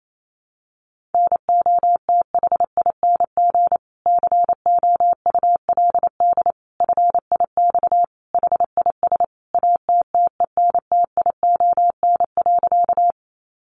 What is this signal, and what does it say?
Morse code, heard as a keyed tone, about 28 wpm, 700 Hz. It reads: NOTHING COULD FIX HIS ATTENTION.
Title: Sentences from Top 1400 Words 28wpm